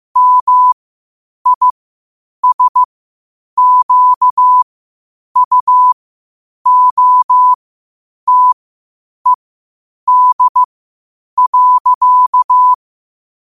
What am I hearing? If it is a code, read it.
Morse code, heard as a keyed tone, about 15 wpm, spaced as for 8 wpm, 1000 Hz. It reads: MISQUOTED.
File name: qotdfiles / 8wpm